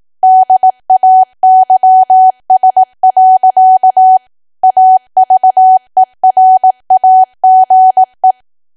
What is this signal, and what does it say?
Morse code, heard as a keyed tone, about 18 wpm, 750 Hz. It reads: DAYS. AVERAGE
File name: cwa / W1AW-Level2 / 18WPM